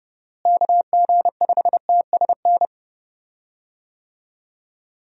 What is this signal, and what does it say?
Morse code, heard as a keyed tone, about 31 wpm, 700 Hz. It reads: KG5TSN
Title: USA Call Signs - 6 Characters - Repeated 2x - 30wpm